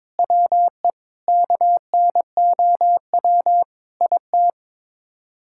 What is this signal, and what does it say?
Morse code, heard as a keyed tone, about 22 wpm, 700 Hz. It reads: WE KNOW IT
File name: practice / 22wpm / Sets of 8 Words - 22wpm